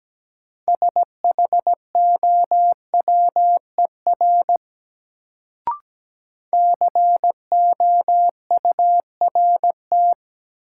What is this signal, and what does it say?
Morse code, heard as a keyed tone, about 17 wpm, 700 Hz. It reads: SHOWER COURT
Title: Sentences from Top 1600 Words - New Word Drill Alt - 17wpm